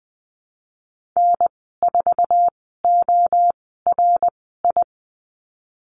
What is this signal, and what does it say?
Morse code, heard as a keyed tone, about 20 wpm, spaced as for 14 wpm, 700 Hz. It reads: N4ORI
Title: 088 Lesson 15g - Number 5 - Call Signs 20wpm